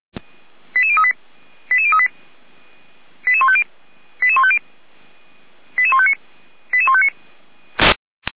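Five-tone selective call (ZVEI), heard as quick runs of five tones, tones of about 70 ms. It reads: 88038 88038 88150 88150 88159 88159
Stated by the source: ZVEI five-tone call received with an SDR.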